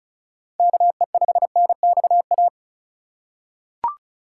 Morse code, heard as a keyed tone, about 36 wpm, 700 Hz. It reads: KE5NXA
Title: USA Call Signs - 6 Characters - 2x Character Spacing - 35wpm